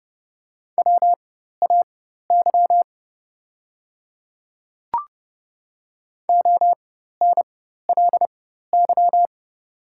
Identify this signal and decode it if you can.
Morse code, heard as a keyed tone, about 30 wpm, spaced as for 14 wpm, 700 Hz. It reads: WAY ONLY
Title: 4x Character Spacing - Top 200 Words - 30wpm